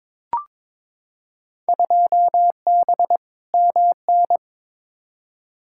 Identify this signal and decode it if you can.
Morse code, heard as a keyed tone, about 22 wpm, 700 Hz. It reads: 2B MN
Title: Field Day Exchanges - 22wpm